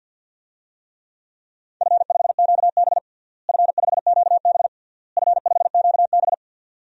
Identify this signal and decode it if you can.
Morse code, heard as a keyed tone, about 50 wpm, 700 Hz. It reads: F5QB F5QB F5QB
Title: DXCC-100 Call Signs - 4 Characters - Repeated 3x - 50wpm